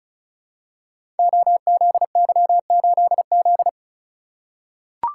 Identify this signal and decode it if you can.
Morse code, heard as a keyed tone, about 36 wpm, 700 Hz. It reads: OZY8Z
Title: Code Groups - 5 Letter:Numbers - 35wpm